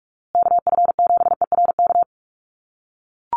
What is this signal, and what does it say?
Morse code, heard as a keyed tone, about 44 wpm, 690 Hz. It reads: KF7ERK
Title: USA Call Signs Repeated 3x 45wpm